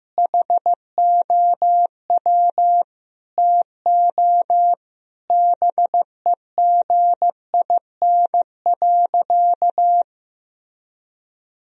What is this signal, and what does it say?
Morse code, heard as a keyed tone, about 15 wpm, 700 Hz. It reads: HOW TO BEGIN.
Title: Sentences from Top 1000 Words - Review Part 1 - 15wpm